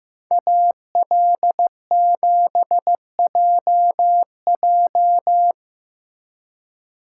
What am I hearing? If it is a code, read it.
Morse code, heard as a keyed tone, about 16 wpm, 700 Hz. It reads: AL7JJ
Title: Speed Racing - USA 2x2 Call Signs 15wpm